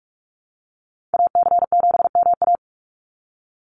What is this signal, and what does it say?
Morse code, heard as a keyed tone, about 44 wpm, 700 Hz. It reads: AC7MA
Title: NA SOTA Activator Callsigns 45wpm